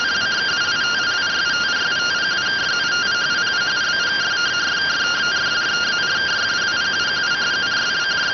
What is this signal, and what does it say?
Two-tone teleprinter signal, about 170 Hz shift.